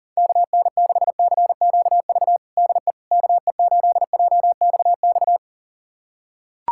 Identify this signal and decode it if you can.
Morse code, heard as a keyed tone, about 40 wpm, 700 Hz. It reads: KN6CQV DE KE8JXX
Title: USA Callsign DE Callsign - 6 Characters - 40wpm